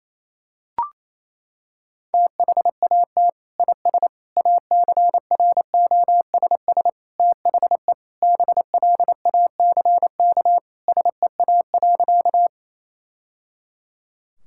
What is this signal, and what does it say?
Morse code, heard as a keyed tone, about 28 wpm, 700 Hz. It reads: THAT IS ACROSS THE BLACK SEA.
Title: Sentences from Top 1000 Words - Review Part 5 - 28wpm